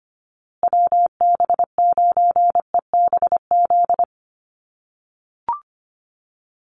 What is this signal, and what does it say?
Morse code, heard as a keyed tone, about 25 wpm, 700 Hz. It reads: WB9EBZ
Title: USA Call Signs Repeated 3x 25wpm